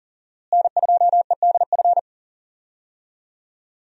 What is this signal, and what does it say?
Morse code, heard as a keyed tone, about 40 wpm, 700 Hz. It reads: N2EDF
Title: USA Call Signs - 5 Characters - Repeated 3x - 40wpm